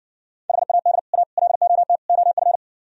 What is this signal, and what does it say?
Morse code, heard as a keyed tone, about 60 wpm, 700 Hz. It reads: HAD A LOT OF